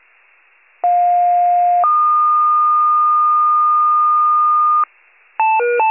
Two-tone page, heard as a one-second tone, then a three-second tone, about 710 then 1220 Hz. It is heard from 0.8 to 4.8 s.
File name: pager / 46 RESCUE / wcalert